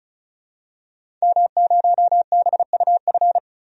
Morse code, heard as a keyed tone, about 36 wpm, 700 Hz. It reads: M0BUF
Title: DXCC-100 Call Signs - 5 Characters - 35wpm